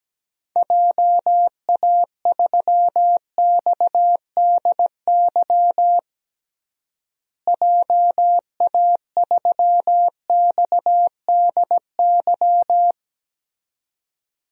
Morse code, heard as a keyed tone, about 17 wpm, 700 Hz. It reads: JA3XDY JA3XDY
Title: DXCC-100 Call Signs - 6 Characters - Repeated 2x - 17wpm